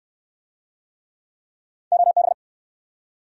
Morse code, heard as a keyed tone, about 60 wpm, 700 Hz. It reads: OB